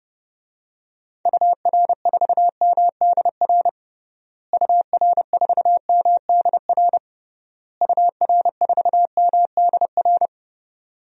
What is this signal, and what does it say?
Morse code, heard as a keyed tone, about 32 wpm, 700 Hz. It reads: UR4MDR UR4MDR UR4MDR
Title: DXCC-100 Call Signs - 6 Characters - Repeated 3x - 30wpm